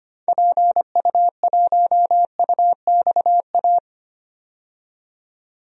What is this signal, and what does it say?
Morse code, heard as a keyed tone, about 25 wpm, 700 Hz. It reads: PU1UXA
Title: DXCC-100 Call Signs - 6 Characters - 25wpm